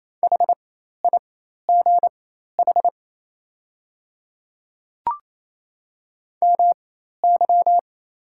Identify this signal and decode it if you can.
Morse code, heard as a keyed tone, about 28 wpm, spaced as for 13 wpm, 700 Hz. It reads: HIGH MY
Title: 4x Character Spacing - Top 200 Words - 28wpm